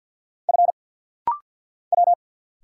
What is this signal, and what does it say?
Morse code, heard as a keyed tone, about 49 wpm, 700 Hz. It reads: F W
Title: Single Letter-Number Alt - Rapid-Fire 50wpm